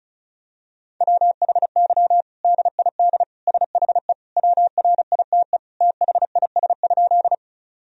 Morse code, heard as a keyed tone, about 36 wpm, 700 Hz. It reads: WHY DID SHE WRITE THIS?
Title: Sentences from Top 1000 Words - Review Part 1 - 35wpm